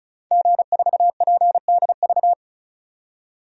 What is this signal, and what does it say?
Morse code, heard as a keyed tone, about 36 wpm, 700 Hz. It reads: G4PDV
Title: DXCC-100 Call Signs - 5 Characters - 35wpm